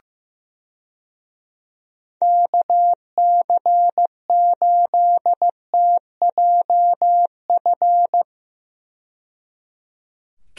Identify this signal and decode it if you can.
Morse code, heard as a keyed tone, about 15 wpm, 700 Hz. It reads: KC8TJF
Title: USA Call Signs - 6 Characters - 15wpm